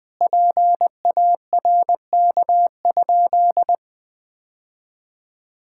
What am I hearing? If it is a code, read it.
Morse code, heard as a keyed tone, about 20 wpm, 700 Hz. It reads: PARK?
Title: Sentences with 4-Word Phrases - 20wpm